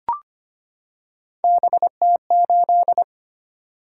Morse code, heard as a keyed tone, about 25 wpm, 700 Hz. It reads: BT8
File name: Three Letter-Number Combinations - 25wpm